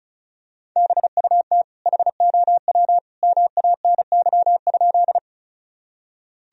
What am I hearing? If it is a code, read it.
Morse code, heard as a keyed tone, about 36 wpm, 700 Hz. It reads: BUT HOW MANY?
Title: Sentences from Top 1000 Words - Review Part 1 - 35wpm